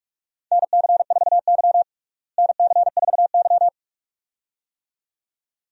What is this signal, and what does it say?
Morse code, heard as a keyed tone, about 46 wpm, 700 Hz. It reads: NC4Y NC4Y
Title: USA Call Signs - 4 Characters - Repeated 2x - 45wpm